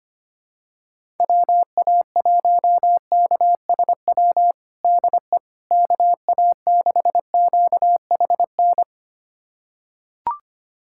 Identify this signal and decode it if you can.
Morse code, heard as a keyed tone, about 25 wpm, 700 Hz. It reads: WA1KSW DE KA6QHN
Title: USA Callsign DE Callsign - 6 Characters - 25wpm